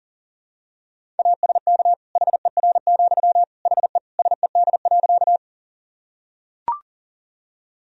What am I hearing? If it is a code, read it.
Morse code, heard as a keyed tone, about 40 wpm, 700 Hz. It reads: ASK HER, HE SED.
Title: Sentences from Top 200 Words and Abbreviations 40wpm